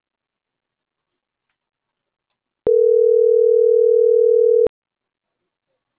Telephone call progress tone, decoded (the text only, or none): ringback tone